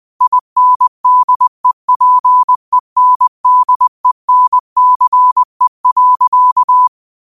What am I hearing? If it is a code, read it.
Morse code, heard as a keyed tone, about 20 wpm, 1000 Hz. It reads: INDEPENDENCE.
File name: qotdfiles / 20wpm